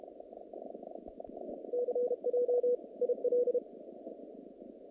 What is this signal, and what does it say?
Morse code, heard as a keyed tone, about 31 wpm, 500 Hz. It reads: CJ IL
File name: contests / naqp-cw-k5tr-op-k5pi- -2100